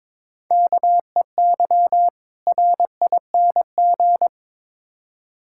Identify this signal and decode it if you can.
Morse code, heard as a keyed tone, about 22 wpm, 700 Hz. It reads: KEY RING
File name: Difficult Words with Hints - 22wpm